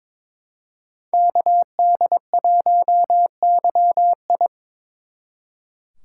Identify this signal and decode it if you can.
Morse code, heard as a keyed tone, about 26 wpm, 700 Hz. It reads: KD1YI
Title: Speed Racing - USA Call Signs 22wpm